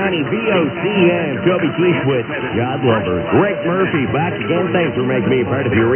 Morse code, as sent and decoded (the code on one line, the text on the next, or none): none